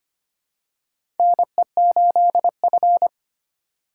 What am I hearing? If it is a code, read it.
Morse code, heard as a keyed tone, about 25 wpm, 700 Hz. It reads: NE8F
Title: USA Call Signs - 4 Characters - 3x Character Spacing - 25wpm